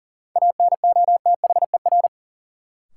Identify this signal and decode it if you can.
Morse code, heard as a keyed tone, about 40 wpm, 700 Hz. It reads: ANOTHER
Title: Sentences from Top 600 Words - New Word Drill - 40wpm